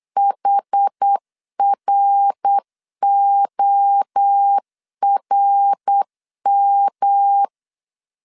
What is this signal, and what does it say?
Morse code, heard as a keyed tone, about 8 wpm, 800 Hz. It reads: HRORM
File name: CW / DET / 1 3 1500C S A 8wpm 4229